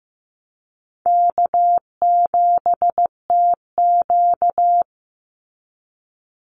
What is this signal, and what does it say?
Morse code, heard as a keyed tone, about 15 wpm, 700 Hz. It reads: K7TQ
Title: NA SOTA Chaser Callsigns 15wpm